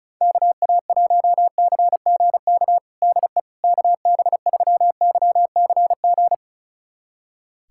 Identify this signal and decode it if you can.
Morse code, heard as a keyed tone, about 36 wpm, 700 Hz. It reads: KA1CGK DE KB3YCG
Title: USA Callsign DE Callsign - 6 Characters - 35wpm